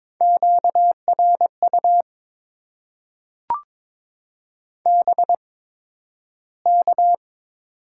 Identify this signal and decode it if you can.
Morse code, heard as a keyed tone, about 22 wpm, spaced as for 12 wpm, 700 Hz. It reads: QRU BK
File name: Top 90 QSO Elements - 8x Character Spacing - 22wpm